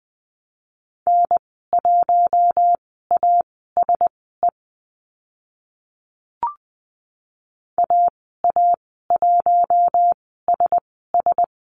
Morse code, heard as a keyed tone, about 20 wpm, spaced as for 14 wpm, 700 Hz. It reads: N1ASE AA1SS
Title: 042 Lesson 8g - Number 1 - Call Signs 20wpm